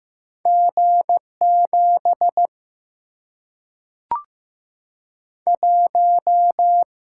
Morse code, heard as a keyed tone, about 15 wpm, 700 Hz. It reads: G7 1
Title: Single-Letter Single-Number 15wpm